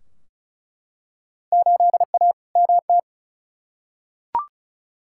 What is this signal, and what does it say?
Morse code, heard as a keyed tone, about 36 wpm, 700 Hz. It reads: 8A MT
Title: Field Day Exchanges Encore - 35wpm